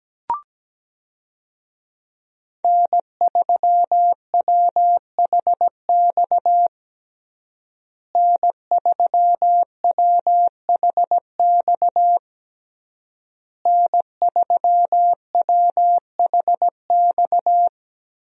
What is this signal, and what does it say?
Morse code, heard as a keyed tone, about 17 wpm, 700 Hz. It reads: N3WHX N3WHX N3WHX
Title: USA Call Signs - 5 Characters - Repeated 3x - 17wpm